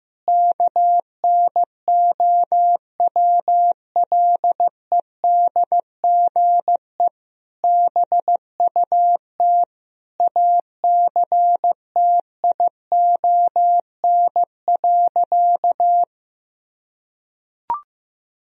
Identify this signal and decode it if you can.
Morse code, heard as a keyed tone, about 15 wpm, 700 Hz. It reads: KNOWLEDGE BUT ACTION.
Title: Sentences from Top 1000 Words - Review Part 4 - 15wpm